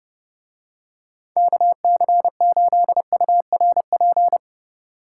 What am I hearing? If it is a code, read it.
Morse code, heard as a keyed tone, about 29 wpm, 700 Hz. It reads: KC8URP